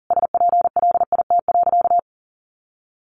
Morse code, heard as a keyed tone, about 40 wpm, 700 Hz. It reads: SPLIT.
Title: Additional Easy Phrases 40wpm